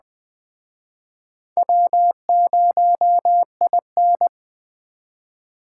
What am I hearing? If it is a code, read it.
Morse code, heard as a keyed tone, about 20 wpm, 700 Hz. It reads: W0IN